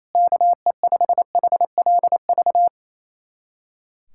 Morse code, heard as a keyed tone, about 28 wpm, 700 Hz. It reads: KE5HLV
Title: USA Callsign DE Callsign - 6 Characters - 28wpm